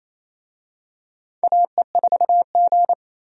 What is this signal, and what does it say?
Morse code, heard as a keyed tone, about 27 wpm, 700 Hz. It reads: AE4G
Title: USA Call Signs - 4 Characters - 28wpm